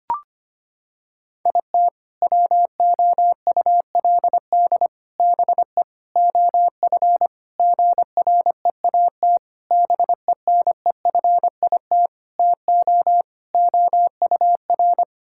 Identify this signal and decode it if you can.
Morse code, heard as a keyed tone, about 25 wpm, 700 Hz. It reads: IT WOULD BE OF GREAT BENEFIT TO OUR
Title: Sentences from Top 1000 Words - Review Part 6 - 25wpm